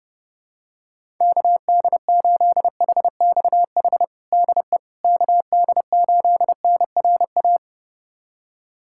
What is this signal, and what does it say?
Morse code, heard as a keyed tone, about 32 wpm, 700 Hz. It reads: KD8HXH DE KD8NRA
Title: USA Callsign DE Callsign 30wpm